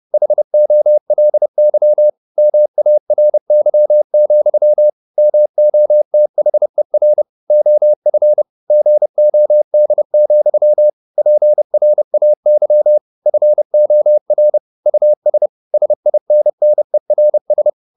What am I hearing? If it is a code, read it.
Morse code, heard as a keyed tone, about 30 wpm, 600 Hz. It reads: HOLY MARY, MOTHER OF GOD, PRAY FOR US SINNERS
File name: the old man and the sea / the old man and the sea 1 30wpm